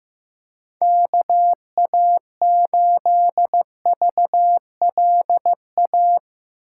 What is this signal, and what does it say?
Morse code, heard as a keyed tone, about 15 wpm, 700 Hz. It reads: KA8VLA
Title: USA Call Signs - 6 Characters - 2x Character Spacing - 15wpm